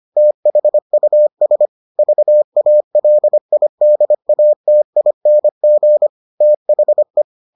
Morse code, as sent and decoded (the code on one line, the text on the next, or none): - .... ..- ... / ...- .- .-.. .. -.. .- - .. -. --. / - .... .
THUS VALIDATING THE